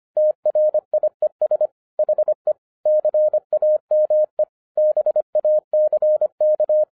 Morse code, heard as a keyed tone, about 25 wpm, 610 Hz. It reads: TRIES HE CAME BACK
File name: FIRST FISTS CONTACT-25wpm